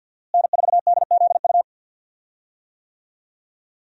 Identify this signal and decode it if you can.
Morse code, heard as a keyed tone, about 46 wpm, 700 Hz. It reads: N4DZU
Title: USA Call Signs - 5 Characters - Repeated 2x - 50wpm